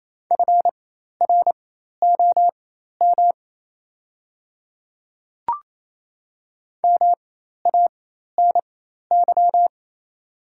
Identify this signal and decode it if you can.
Morse code, heard as a keyed tone, about 27 wpm, spaced as for 13 wpm, 700 Hz. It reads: FROM MANY
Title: 4x Character Spacing - Top 100 Words - 28wpm